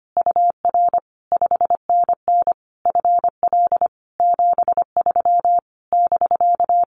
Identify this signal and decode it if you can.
Morse code, heard as a keyed tone, about 25 wpm, 700 Hz. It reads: UR 5NN FL 73 <BK>
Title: Common QSO Phrases Encore 25wpm